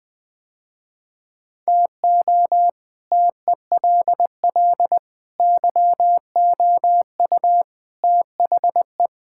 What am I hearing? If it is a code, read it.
Morse code, heard as a keyed tone, about 20 wpm, 700 Hz. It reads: TO TELL YOU THE
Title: Sentences with 5-Word Phrases - Spoken Beginning - 20wpm